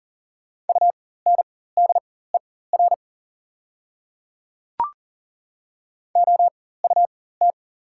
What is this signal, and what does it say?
Morse code, heard as a keyed tone, about 38 wpm, spaced as for 19 wpm, 700 Hz. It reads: UNDER OUT
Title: 4x Character Spacing - Top 200 Words - 40wpm